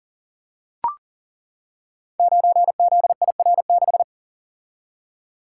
Morse code, heard as a keyed tone, about 39 wpm, 700 Hz. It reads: 9ZIR6
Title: Code Groups - 5 Letter:Numbers - 40wpm